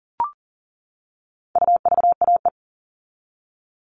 Morse code, heard as a keyed tone, about 40 wpm, 700 Hz. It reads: UVAE
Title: All Cut Numbers - 4-Digit 40wpm